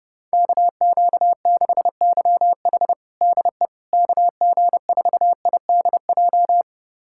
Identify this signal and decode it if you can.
Morse code, heard as a keyed tone, about 30 wpm, 700 Hz. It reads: KQ6YH DE KG4IDJ
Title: USA Callsign DE Callsign 30wpm